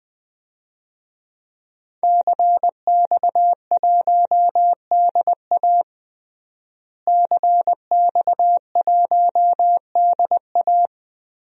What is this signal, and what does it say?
Morse code, heard as a keyed tone, about 20 wpm, 700 Hz. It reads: CX1DA CX1DA
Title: DXCC-100 Call Signs - 5 Characters - Repeated 3x - 20wpm